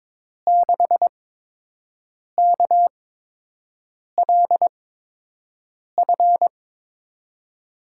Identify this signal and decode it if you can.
Morse code, heard as a keyed tone, about 22 wpm, spaced as for 6 wpm, 700 Hz. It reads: 6KLF
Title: USA Call Signs - 8x Character Spacing - 22wpm